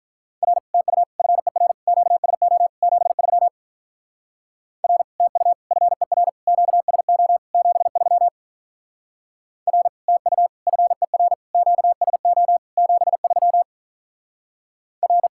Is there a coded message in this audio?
Yes: Morse code from 0.4 to 15.3 s, heard as a keyed tone, about 45 wpm, 700 Hz. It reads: R TU FER QSO 73 R TU FER QSO 73 R TU FER QSO 73 R